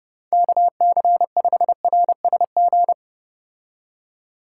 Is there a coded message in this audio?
Yes: Morse code from 0.3 to 2.9 s, heard as a keyed tone, about 30 wpm, 700 Hz. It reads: KC5RSG